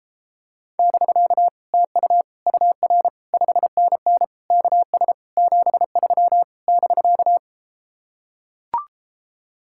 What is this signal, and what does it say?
Morse code, heard as a keyed tone, about 33 wpm, 700 Hz. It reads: <BK> TU UR 5NN KS 73 <BK>